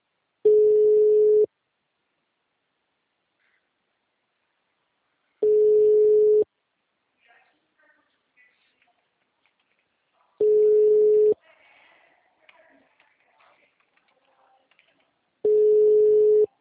Ringback tone (Europe, 425 Hz).